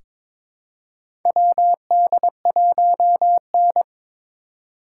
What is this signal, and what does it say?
Morse code, heard as a keyed tone, about 21 wpm, 700 Hz. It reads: WD1N
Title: USA Call Signs - 2x Character Spacing - 22wpm